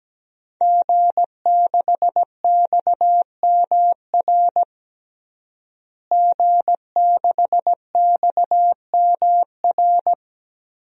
Morse code, heard as a keyed tone, about 17 wpm, 700 Hz. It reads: G6XMR G6XMR
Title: DXCC-100 Call Signs - 5 Characters - Repeated 2x - 17wpm